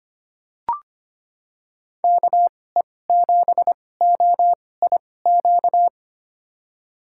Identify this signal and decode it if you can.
Morse code, heard as a keyed tone, about 25 wpm, spaced as for 18 wpm, 700 Hz. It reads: KE7OIQ
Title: USA Call Signs - 6 Characters - 2x Character Spacing - 25wpm